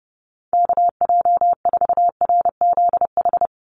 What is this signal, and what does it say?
Morse code, heard as a keyed tone, about 30 wpm, 700 Hz. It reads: KJ4RZH
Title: USA Call Signs 30wpm